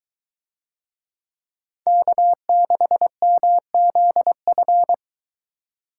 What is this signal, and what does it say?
Morse code, heard as a keyed tone, about 23 wpm, 700 Hz. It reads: K6MZF